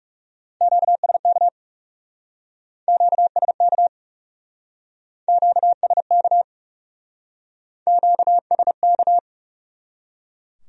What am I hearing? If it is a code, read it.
Morse code, heard as a keyed tone, about 38 wpm, 700 Hz. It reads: QSK QSK QSK QSK